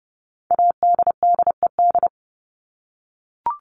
Morse code, heard as a keyed tone, about 30 wpm, 700 Hz. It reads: ADDED
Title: ED Suffix - Top 100 Words with Base Words 30wpm